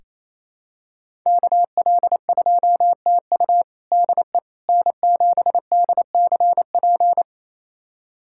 Morse code, heard as a keyed tone, about 28 wpm, 700 Hz. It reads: KL2TU DE N7DCP
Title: USA Callsign DE Callsign - 5 Characters - 28wpm